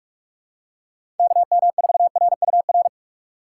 Morse code, heard as a keyed tone, about 44 wpm, 700 Hz. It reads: KM4RUR
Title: USA Call Signs - 6 Characters - 3x Character Spacing - 45wpm